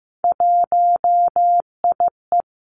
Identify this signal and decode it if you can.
Morse code, heard as a keyed tone, about 15 wpm, 700 Hz. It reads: 1IE
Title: Three Challenging Letter-Number Combinations 15wpm